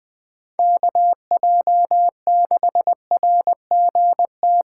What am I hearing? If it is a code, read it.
Morse code, heard as a keyed tone, about 20 wpm, 700 Hz. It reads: KJ6RGT
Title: Speed Racing - USA 2x3 Call Signs 17wpm